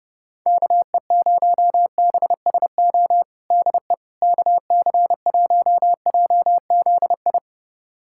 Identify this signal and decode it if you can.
Morse code, heard as a keyed tone, about 32 wpm, 700 Hz. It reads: KE0BSO DE KC1JZI
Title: USA Callsign DE Callsign - 6 Characters - 30wpm